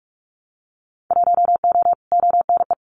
Morse code, heard as a keyed tone, about 44 wpm, 700 Hz. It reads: 1O ONE
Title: Winter Field Day Exchanges 45wpm